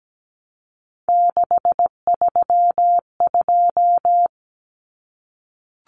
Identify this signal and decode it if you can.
Morse code, heard as a keyed tone, about 17 wpm, 700 Hz. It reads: 632
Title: Common Cut Numbers - 3-Digit 17wpm